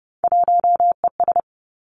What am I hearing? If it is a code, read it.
Morse code, heard as a keyed tone, about 30 wpm, 700 Hz. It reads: 1ES
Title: Three Challenging Letter-Number Combinations 30wpm